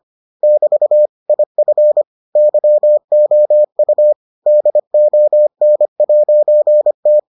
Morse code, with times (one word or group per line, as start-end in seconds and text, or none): 0.4-1.1 <BT>
1.3-2.0 IF
2.4-4.1 YOU
4.5-7.2 DON'T